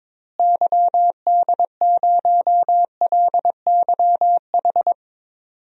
Morse code, heard as a keyed tone, about 22 wpm, 700 Hz. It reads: YD0LYH
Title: DXCC-100 Call Signs - 6 Characters - Repeated 2x - 22wpm